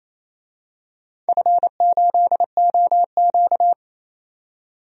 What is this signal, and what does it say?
Morse code, heard as a keyed tone, about 28 wpm, 700 Hz. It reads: F8OQ
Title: DXCC-100 Call Signs - 4 Characters - 28wpm